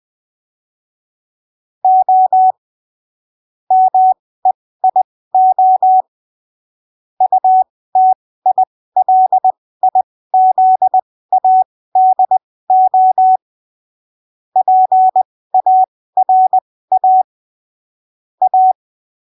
Morse code, heard as a keyed tone, about 20 wpm, spaced as for 15 wpm, 750 Hz. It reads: O MEIO UTILIZADO PARA A